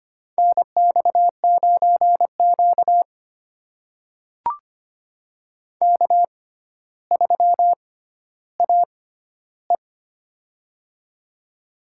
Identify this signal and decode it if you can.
Morse code, heard as a keyed tone, about 25 wpm, spaced as for 13 wpm, 700 Hz. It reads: NX9Q K3AE
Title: USA Call Signs - 4 Characters - 6x Character Spacing - 25wpm